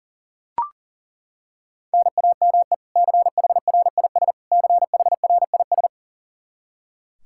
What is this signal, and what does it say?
Morse code, heard as a keyed tone, about 37 wpm, 700 Hz. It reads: NAME CHRIS CHRIS
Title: Practice QSO - Exchange 1 - Beginning - 40wpm